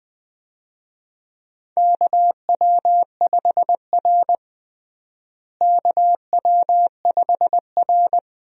Morse code, heard as a keyed tone, about 20 wpm, 700 Hz. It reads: KW5R KW5R
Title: USA Call Signs - 4 Characters - Repeated 3x - 20wpm